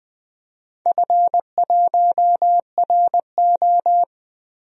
Morse code, heard as a keyed tone, about 20 wpm, 700 Hz. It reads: F1RO
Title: DXCC-100 Call Signs - 4 Characters - 20wpm